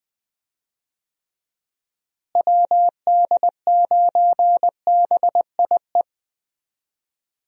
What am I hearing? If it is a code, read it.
Morse code, heard as a keyed tone, about 19 wpm, 700 Hz. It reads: WD9BIE
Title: USA Call Signs - 6 Characters - Repeated 3x - 20wpm